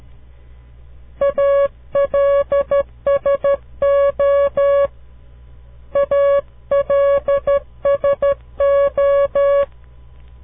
Morse code, heard as a keyed tone, about 13 wpm, 560 Hz. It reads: ALSO ALSO
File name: CW Learning Programs / farnsworth / unit 2